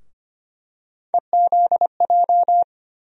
Morse code, heard as a keyed tone, about 25 wpm, 700 Hz. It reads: EZJ
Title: USA Call Sign Postfixes - 3 Letters 25wpm